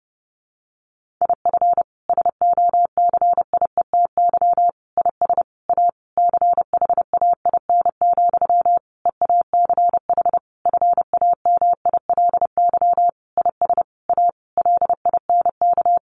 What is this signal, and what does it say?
Morse code, heard as a keyed tone, about 28 wpm, 700 Hz. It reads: IF SOCIETY IS A CHAIN, EACH FAMILY IS A LINK
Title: Sentences from Top 1500 Words 30wpm